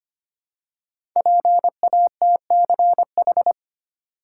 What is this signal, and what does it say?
Morse code, heard as a keyed tone, about 25 wpm, 700 Hz. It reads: PATCH
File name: 5 Letter Words - First 3 Letters Given - 25wpm